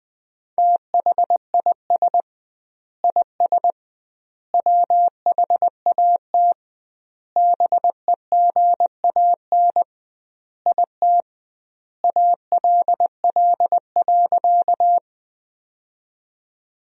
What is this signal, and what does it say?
Morse code, heard as a keyed tone, about 21 wpm, 700 Hz. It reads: THIS IS WHAT BEGAN IT ALL.
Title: 2x Word-Spacing - Sentences from Top 300 Words - 20wpm